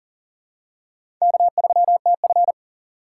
Morse code, heard as a keyed tone, about 38 wpm, 700 Hz. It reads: K3TF
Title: K1USN SST QSO Practice - 40wpm